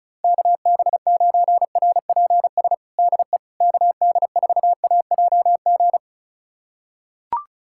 Morse code, heard as a keyed tone, about 34 wpm, 690 Hz. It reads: KB9RPS DE KD4AJG
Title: USA Callsign DE Callsign - 6 Characters - 35wpm